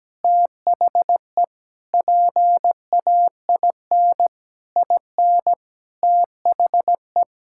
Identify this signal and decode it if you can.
Morse code, heard as a keyed tone, about 17 wpm, 700 Hz. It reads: THE PAIN IN THE
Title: Sentences from Top 1600 Words 17wpm